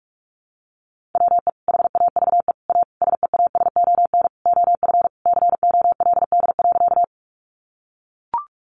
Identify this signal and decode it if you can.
Morse code, heard as a keyed tone, about 44 wpm, 700 Hz. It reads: WE HAVE A SEASON OF COLD.
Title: Sentences from Top 1400 Words 45wpm